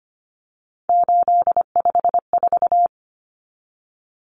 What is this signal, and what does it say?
Morse code, heard as a keyed tone, about 25 wpm, 700 Hz. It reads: 854
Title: Common Cut Numbers - 3-Digit 25wpm